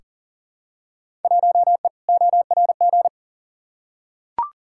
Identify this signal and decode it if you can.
Morse code, heard as a keyed tone, about 38 wpm, 700 Hz. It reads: 1E ORG
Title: Field Day Exchanges - 40wpm